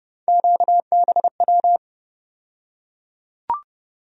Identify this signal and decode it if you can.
Morse code, heard as a keyed tone, about 32 wpm, 700 Hz. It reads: QBW
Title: USA Call Sign Postfixes - 3 Letters 30wpm